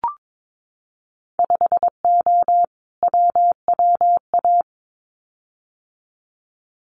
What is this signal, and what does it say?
Morse code, heard as a keyed tone, about 22 wpm, 700 Hz. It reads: 5O WWA
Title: Winter Field Day Exchanges Encore 22wpm